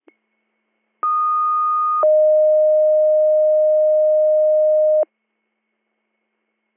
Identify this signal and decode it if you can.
Two-tone page, heard as a one-second tone, then a three-second tone, about 1230 then 620 Hz. It is heard from 1.0 to 5.0 s.